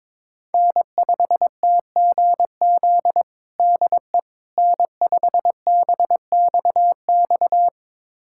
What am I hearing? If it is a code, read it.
Morse code, heard as a keyed tone, about 22 wpm, 700 Hz. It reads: N5TGZ DE N5BXX
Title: USA Callsign DE Callsign - 5 Characters - 22wpm